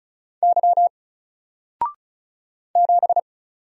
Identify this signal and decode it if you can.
Morse code, heard as a keyed tone, about 34 wpm, 700 Hz. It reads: Y 7
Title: Single Letter-Number Alt 35wpm